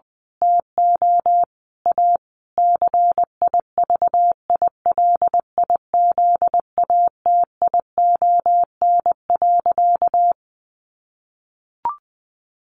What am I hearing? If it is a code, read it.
Morse code, heard as a keyed tone, about 20 wpm, 700 Hz. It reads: TO A CIVILIZATION.
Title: A Sound Of Thunder 20wpm